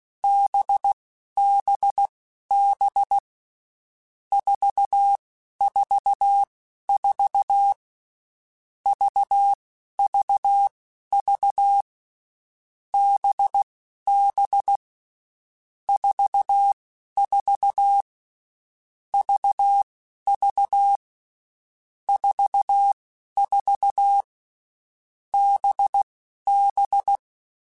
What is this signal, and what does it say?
Morse code, heard as a keyed tone, about 16 wpm, spaced as for 11 wpm, 800 Hz. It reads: BBB 444 VVV BB 44 VV 44 BB
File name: Groups@10 / Groups@10 mp3s / VB46@10